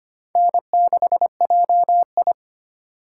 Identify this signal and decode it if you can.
Morse code, heard as a keyed tone, about 31 wpm, 700 Hz. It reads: N6JI